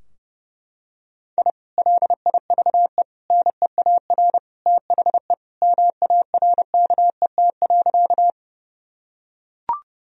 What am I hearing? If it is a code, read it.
Morse code, heard as a keyed tone, about 32 wpm, 700 Hz. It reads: I LIVE NEAR THE MARKET.
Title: Sentences from Top 1000 Words - Review Part 3 - 30wpm